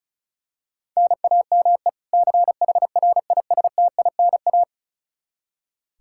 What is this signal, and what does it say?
Morse code, heard as a keyed tone, about 36 wpm, 700 Hz. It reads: NAME CHRISTINA
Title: Practice QSO - Exchange 1 - Beginning - 35wpm